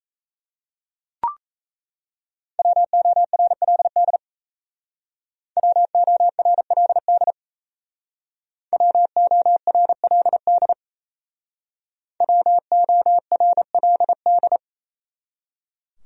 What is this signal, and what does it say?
Morse code, heard as a keyed tone, about 35 wpm, 700 Hz. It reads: WORLD WORLD WORLD WORLD